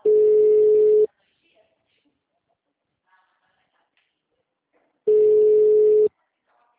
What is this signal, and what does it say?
Ringback tone (Europe, 425 Hz), with speech.